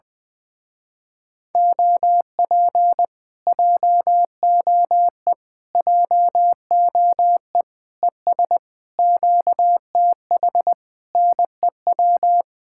Morse code, heard as a keyed tone, about 20 wpm, 700 Hz. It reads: OP JOE JOE ES QTH NEW
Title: Speed-Racing Common QSO Phrases Encore - 20wpm